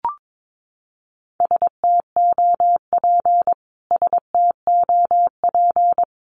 Morse code, heard as a keyed tone, about 22 wpm, 700 Hz. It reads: STOP STOP
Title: Top 500 Words Repeated 22wpm